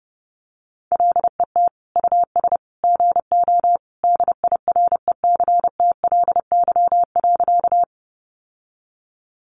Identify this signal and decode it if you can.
Morse code, heard as a keyed tone, about 30 wpm, 700 Hz. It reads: LET US GO DIRECTLY.
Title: Sentences from Top 800 Words 30wpm